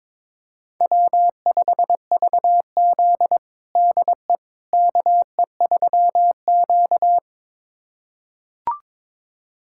Morse code, heard as a keyed tone, about 21 wpm, 700 Hz. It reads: W5VZ DE KE3Q
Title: USA Callsign DE Callsign - 4 Characters - 22wpm